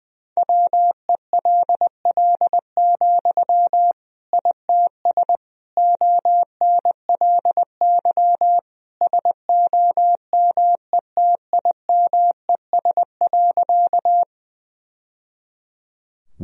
Morse code, heard as a keyed tone, about 20 wpm, 700 Hz. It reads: WELL, ITS ONLY SOMETIMES.
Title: Sentences from Top 1000 Words - Review Part 3 - 20wpm